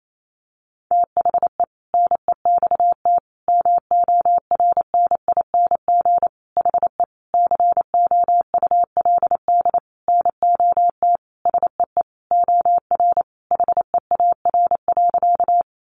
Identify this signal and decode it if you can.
Morse code, heard as a keyed tone, about 28 wpm, 700 Hz. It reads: THE NEXT MORNING HE COULD NOT SEE OR HEAR.
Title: Sentences from Top 600 Words 28wpm